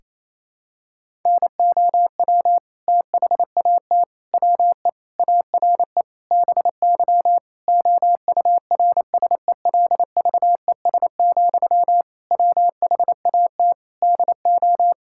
Morse code, heard as a keyed tone, about 28 wpm, 700 Hz. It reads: NOW THAT WE ARE BY OURSELVES, WHAT DO
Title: Sentences from Top 1600 Words 28wpm